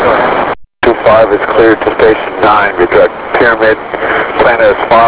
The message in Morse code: none